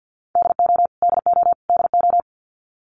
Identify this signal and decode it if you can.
Morse code, heard as a keyed tone, about 50 wpm, 700 Hz. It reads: DO DO DO
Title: Top 300 Words Repeated 50wpm